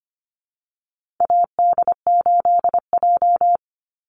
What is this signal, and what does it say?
Morse code, heard as a keyed tone, about 25 wpm, 700 Hz. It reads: AD8J
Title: Call Signs - 25wpm